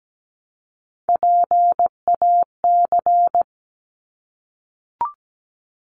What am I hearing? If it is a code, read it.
Morse code, heard as a keyed tone, about 17 wpm, 700 Hz. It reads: PAC
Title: ARRL Sections 17wpm